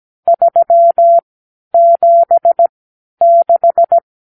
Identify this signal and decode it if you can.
Morse code, heard as a keyed tone, about 17 wpm, spaced as for 11 wpm, 680 Hz. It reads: 376
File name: k7qo / Lesson 52 (8)